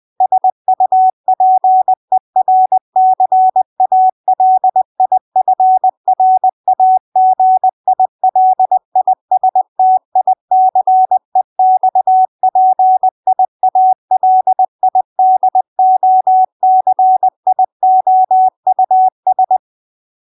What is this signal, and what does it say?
Morse code, heard as a keyed tone, about 20 wpm, 750 Hz. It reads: SUPERCALIFRAGILISTICEXPIALIDOCIOUS
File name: CW Learning Programs / sounds and personalities of cw and transmitters / 11 - Docious20